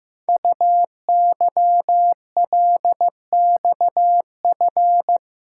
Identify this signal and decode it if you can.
Morse code, heard as a keyed tone, about 15 wpm, 700 Hz. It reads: UYLXF
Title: Code Groups - 5 Letters - 15wpm